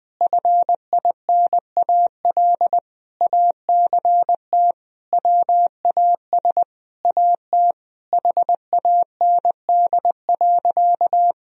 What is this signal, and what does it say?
Morse code, heard as a keyed tone, about 20 wpm, 700 Hz. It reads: FINAL ACT WAS AT HAND.
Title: Sentences from Top 1000 Words - Review Part 2 - 20wpm